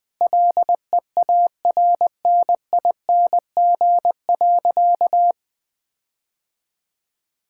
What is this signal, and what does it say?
Morse code, heard as a keyed tone, about 20 wpm, 700 Hz. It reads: LEARNING.
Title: Sentences with 7-Word Phrases - 20wpm